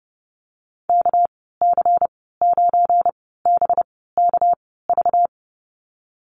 Morse code, heard as a keyed tone, about 30 wpm, spaced as for 17 wpm, 700 Hz. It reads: KC9BKV
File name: Lesson 37g - Letter x - Call Signs 30wpm